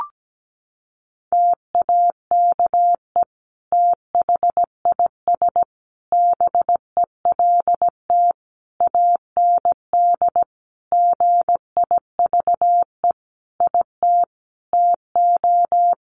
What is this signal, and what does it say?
Morse code, heard as a keyed tone, about 17 wpm, 700 Hz. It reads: TAKE THIS BELT AND GIVE IT TO
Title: Sentences from Top 1400 Words 17wpm